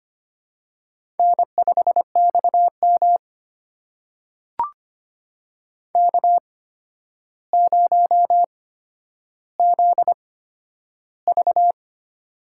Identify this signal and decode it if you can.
Morse code, heard as a keyed tone, about 25 wpm, spaced as for 11 wpm, 700 Hz. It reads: N5XM K0ZV
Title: USA Call Signs - 4 Characters - 8x Character Spacing - 25wpm